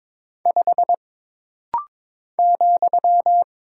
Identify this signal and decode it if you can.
Morse code, heard as a keyed tone, about 22 wpm, 700 Hz. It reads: 5 ,
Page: This rapid-fire timing is geared to push your ICR (Instant Character Recognition) skills.